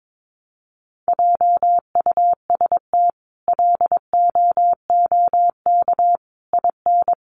Morse code, heard as a keyed tone, about 22 wpm, 700 Hz. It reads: JUST LOOK IN
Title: Sentences from Top 1300 Words 22wpm